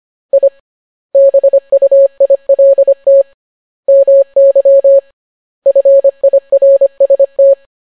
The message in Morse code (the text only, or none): I BUILT MY FIRST